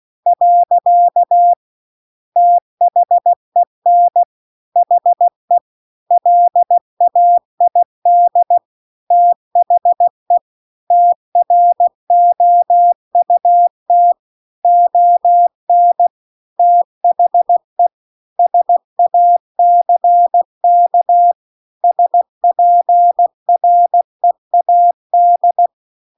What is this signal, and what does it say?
Morse code, heard as a keyed tone, about 16 wpm, 700 Hz. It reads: . THEN HE LAID THE TROUT ON THE SACK SPREAD
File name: CW-The-Big-Twoheaded-River-2-16WPM